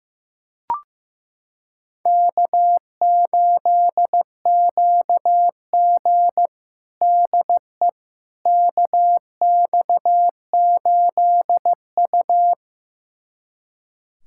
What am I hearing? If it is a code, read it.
Morse code, heard as a keyed tone, about 15 wpm, 700 Hz. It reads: K8QG DE KX8U
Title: USA Callsign DE Callsign - 4 Characters - 15wpm